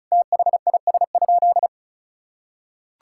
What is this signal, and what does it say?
Morse code, heard as a keyed tone, about 36 wpm, 700 Hz. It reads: THIS?
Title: Sentences from Top 1000 Words - Review Part 5 - 35wpm